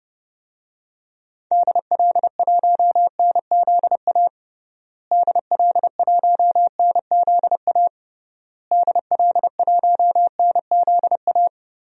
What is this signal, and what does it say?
Morse code, heard as a keyed tone, about 32 wpm, 700 Hz. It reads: DL1NZA DL1NZA DL1NZA
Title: DXCC-100 Call Signs - 6 Characters - Repeated 3x - 30wpm